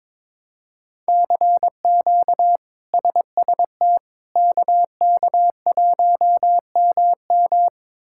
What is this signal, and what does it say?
Morse code, heard as a keyed tone, about 22 wpm, 700 Hz. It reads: CQ SST KK1MM
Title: K1USN SST QSO Practice - 22wpm